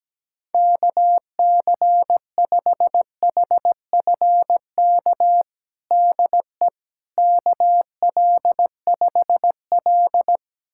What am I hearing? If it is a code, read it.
Morse code, heard as a keyed tone, about 17 wpm, 700 Hz. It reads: KC5HFK DE KL5L
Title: USA Callsign DE Callsign 17wpm